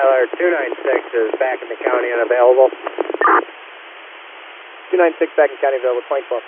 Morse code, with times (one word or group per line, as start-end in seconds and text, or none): none